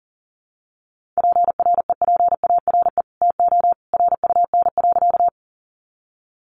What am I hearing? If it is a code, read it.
Morse code, heard as a keyed tone, about 40 wpm, 700 Hz. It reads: PREPARE TO RUN.
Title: Sentences from Top 900 Words 40wpm